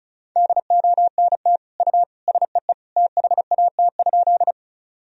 Morse code, heard as a keyed tone, about 34 wpm, 700 Hz. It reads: DONT U SEE THAT?
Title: Sentences from Top 300 Words and Abbreviations 35wpm